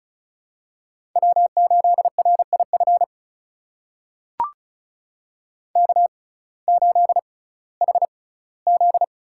Morse code, heard as a keyed tone, about 34 wpm, spaced as for 19 wpm, 700 Hz. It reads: W8RIF K8HZ
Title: USA Call Signs - 5 Characters - 6x Character Spacing - 35wpm